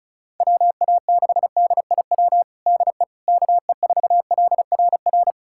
Morse code, heard as a keyed tone, about 36 wpm, 700 Hz. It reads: WA6DIW DE KE4LRR
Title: USA Callsign DE Callsign 35wpm